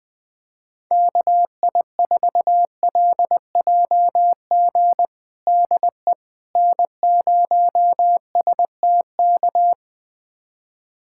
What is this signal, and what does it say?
Morse code, heard as a keyed tone, about 21 wpm, 700 Hz. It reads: KI4LJG DE N0STK
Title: USA Callsign DE Callsign 20wpm